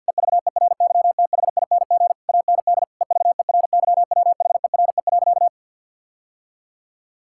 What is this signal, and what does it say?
Morse code, heard as a keyed tone, about 50 wpm, 700 Hz. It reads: EVERYTHING AND EVERYWHERE.
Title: Sentences from Top 1600 Words 50wpm